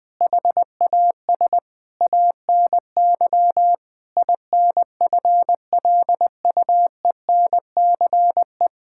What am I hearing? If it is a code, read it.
Morse code, heard as a keyed tone, about 20 wpm, 700 Hz. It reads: HAS ANY INFLUENCE
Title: Sentences from Top 1600 Words 20wpm